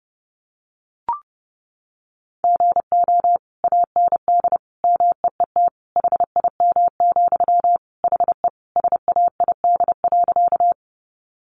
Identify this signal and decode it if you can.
Morse code, heard as a keyed tone, about 30 wpm, 700 Hz. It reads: GO AND MEET HIM, HE SAID.
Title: Sentences from Top 1100 Words 30wpm